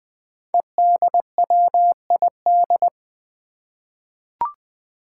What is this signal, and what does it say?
Morse code, heard as a keyed tone, about 20 wpm, 700 Hz. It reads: EDWID
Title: Five Challenging Letter-Number Combinations 20wpm